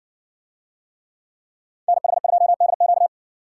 Morse code, heard as a keyed tone, about 60 wpm, 700 Hz. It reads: NH2DQ